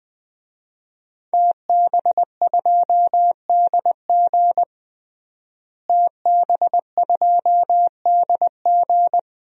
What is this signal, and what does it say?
Morse code, heard as a keyed tone, about 20 wpm, 700 Hz. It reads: TB2DG TB2DG
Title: DXCC-100 Call Signs - 5 Characters - Repeated 2x - 20wpm